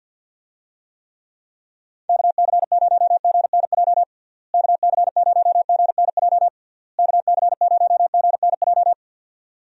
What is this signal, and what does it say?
Morse code, heard as a keyed tone, about 46 wpm, 700 Hz. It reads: KC0GNJ KC0GNJ KC0GNJ
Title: USA Call Signs - 6 Characters - Repeated 3x - 50wpm